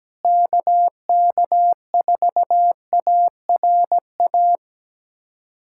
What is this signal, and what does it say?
Morse code, heard as a keyed tone, about 17 wpm, 700 Hz. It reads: KK4ARA
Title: USA Callsign DE Callsign - 6 Characters - 17wpm